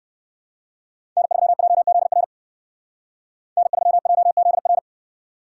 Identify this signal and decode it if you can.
Morse code, heard as a keyed tone, about 61 wpm, 700 Hz. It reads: N3JZR N3JZR